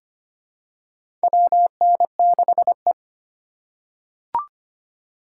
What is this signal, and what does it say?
Morse code, heard as a keyed tone, about 25 wpm, 700 Hz. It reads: WN6E
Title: USA Call Signs - 4 Characters - 6x Character Spacing - 25wpm